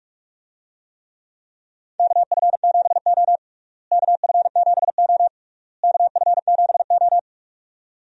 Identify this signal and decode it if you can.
Morse code, heard as a keyed tone, about 46 wpm, 700 Hz. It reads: KF7O KF7O KF7O
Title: USA Call Signs - 4 Characters - Repeated 3x - 45wpm